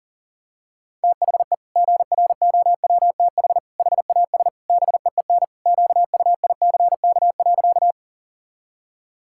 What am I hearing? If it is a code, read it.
Morse code, heard as a keyed tone, about 40 wpm, 700 Hz. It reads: THE GROWTH HAS BEEN QUICK.
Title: Sentences from Top 1000 Words - Review Part 5 - 40wpm